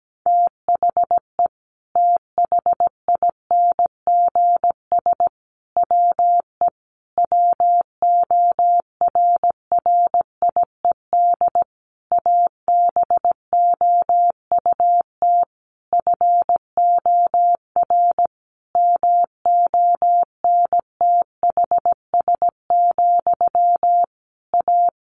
Morse code, heard as a keyed tone, about 17 wpm, 700 Hz. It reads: THE THINGS WE WORRIED ABOUT FOR MONTHS, A
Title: A Sound Of Thunder 17wpm